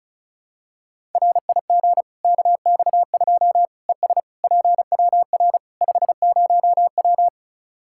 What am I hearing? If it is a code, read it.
Morse code, heard as a keyed tone, about 36 wpm, 700 Hz. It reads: RIG KX2 ES PWR 50W
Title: Practice QSO - Exchange 2 - Info - 35wpm